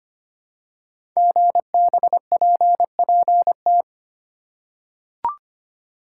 Morse code, heard as a keyed tone, about 25 wpm, 700 Hz. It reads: GBPPT
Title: Code Groups - 5 Letters - 25wpm